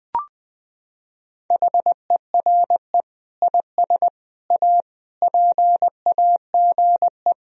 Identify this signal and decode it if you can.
Morse code, heard as a keyed tone, about 20 wpm, 700 Hz. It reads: HERE IS A PAGE
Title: Sentences from Top 1000 Words - Review Part 1 - 20wpm